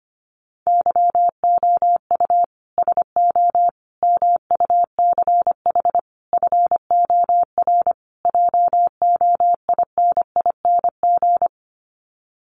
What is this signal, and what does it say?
Morse code, heard as a keyed tone, about 25 wpm, 700 Hz. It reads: YOU SO MUCH FOR JOINING
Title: Sets of 5 Words Encore2 25wpm